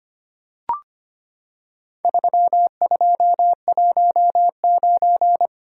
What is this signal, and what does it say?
Morse code, heard as a keyed tone, about 25 wpm, 700 Hz. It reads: 3219
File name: practice / 25wpm / Four Letter-Number Combinations - 25wpm